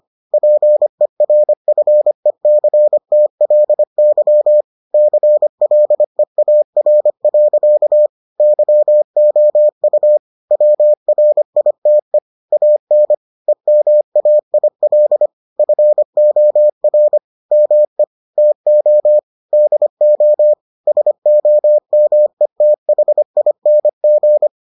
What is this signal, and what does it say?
Morse code, heard as a keyed tone, about 25 wpm, 600 Hz. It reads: PERFECTLY CLEAR. YOU WRITE AN EMAIL FOR ME TO DO SOMETHING